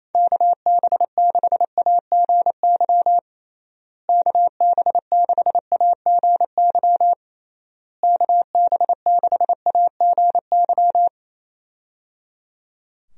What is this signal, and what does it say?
Morse code, heard as a keyed tone, about 27 wpm, 700 Hz. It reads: KB6AGY KB6AGY KB6AGY
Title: USA Call Signs - 6 Characters - Repeated 3x - 28wpm